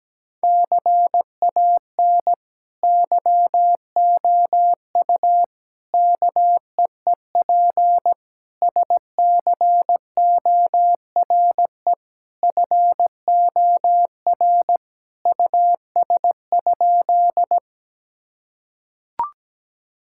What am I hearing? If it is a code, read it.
Morse code, heard as a keyed tone, about 17 wpm, 700 Hz. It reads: CAN YOU KEEP SCORE FOR US?
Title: Sentences from Top 1600 Words 17wpm